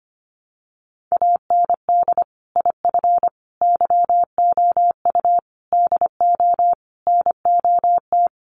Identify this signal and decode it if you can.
Morse code, heard as a keyed tone, about 25 wpm, 700 Hz. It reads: AND IF YOU DO NOT
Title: Sets of 5 Words 25wpm